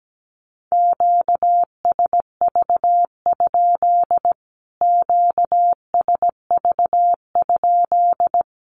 Morse code, heard as a keyed tone, about 17 wpm, 700 Hz. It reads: QSV? QSV?
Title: Q-Codes Repeated 3x 17wpm